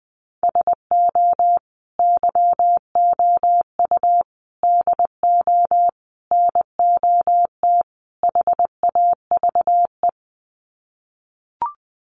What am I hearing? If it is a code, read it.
Morse code, heard as a keyed tone, about 20 wpm, 700 Hz. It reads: SO YOU DO NOT HAVE
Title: Sets of 5 Words Encore 20wpm